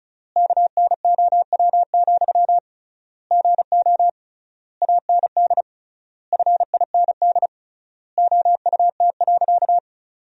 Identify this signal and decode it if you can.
Morse code, heard as a keyed tone, about 36 wpm, 700 Hz. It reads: KNOW, GO AND FIND OUT.
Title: 3x Word-Spacing - Sentences from Top 200 Words - 35wpm